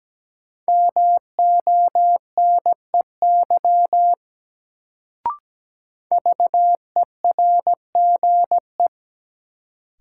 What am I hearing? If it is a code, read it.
Morse code, heard as a keyed tone, about 17 wpm, 700 Hz. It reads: MONEY VERGE
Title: Sets of 4 Words - New Word Drill - 17wpm